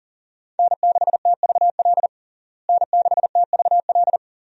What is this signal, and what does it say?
Morse code, heard as a keyed tone, about 39 wpm, 700 Hz. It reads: N6TVL N6TVL
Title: USA Call Signs - 5 Characters - Repeated 2x - 40wpm